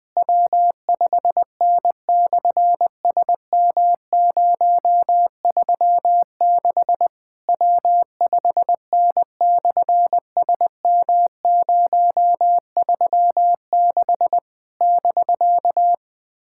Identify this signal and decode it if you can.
Morse code, heard as a keyed tone, about 20 wpm, 700 Hz. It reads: W5N/SM036 W5N/SM036 <BK>
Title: SOTA QSO Practice 20wpm